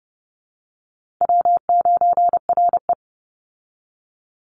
Morse code, heard as a keyed tone, about 30 wpm, 700 Hz. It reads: W9RE
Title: Call Signs Repeated 2x - 30wpm